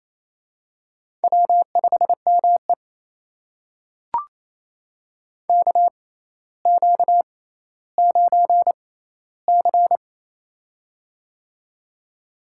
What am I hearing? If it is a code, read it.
Morse code, heard as a keyed tone, about 27 wpm, spaced as for 14 wpm, 700 Hz. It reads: W5ME KQ9C
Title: USA Call Signs - 4 Characters - 6x Character Spacing - 28wpm